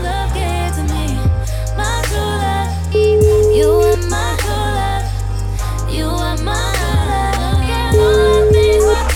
Ringback tone (Europe, 425 Hz), with noise.